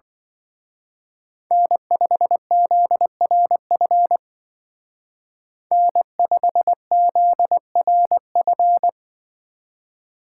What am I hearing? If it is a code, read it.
Morse code, heard as a keyed tone, about 24 wpm, 700 Hz. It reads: N5ZRF N5ZRF
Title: Speed Racing - USA 1x3 Call Signs 20wpm